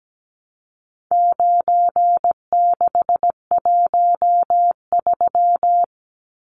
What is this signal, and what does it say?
Morse code, heard as a keyed tone, about 17 wpm, 700 Hz. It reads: 9613
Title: Numbers - 4-Digit 17wpm